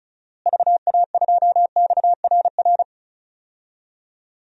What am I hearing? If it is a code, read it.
Morse code, heard as a keyed tone, about 34 wpm, 700 Hz. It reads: VA2XRR
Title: DXCC-100 Call Signs - 6 Characters - Repeated 3x - 35wpm